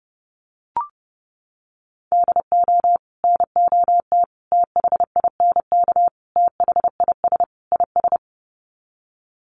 Morse code, heard as a keyed tone, about 28 wpm, 700 Hz. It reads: DO NOT THINK THIS IS
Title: Sets of 5 Words Encore2 30wpm